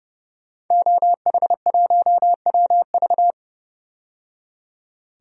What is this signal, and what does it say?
Morse code, heard as a keyed tone, about 32 wpm, 700 Hz. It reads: OH1WV
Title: DXCC-100 Call Signs - 5 Characters - 30wpm